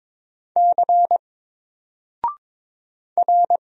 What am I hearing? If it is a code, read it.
Morse code, heard as a keyed tone, about 22 wpm, 700 Hz. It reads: C R
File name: Single-Letter-Number-22wpm